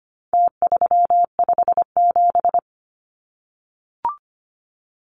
Morse code, heard as a keyed tone, about 25 wpm, 700 Hz. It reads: T357
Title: Numbers - Contest Style - 4-Digit 25wpm